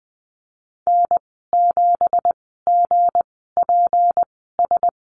Morse code, heard as a keyed tone, about 20 wpm, spaced as for 15 wpm, 700 Hz. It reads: N7GPS
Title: 167 Lesson 27g - Number 7 - Call Signs 20wpm